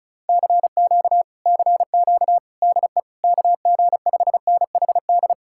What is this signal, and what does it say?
Morse code, heard as a keyed tone, about 47 wpm, spaced as for 41 wpm, 700 Hz. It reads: CQ CQ DE KG5NHD
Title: Speed-Racing Common QSO Phrases - 35wpm